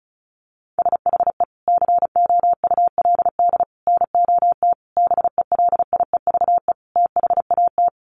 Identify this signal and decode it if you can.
Morse code, heard as a keyed tone, about 34 wpm, 700 Hz. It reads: SHE COULD NOT BELIEVE THAT
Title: Sentences from Top 1000 Words 35wpm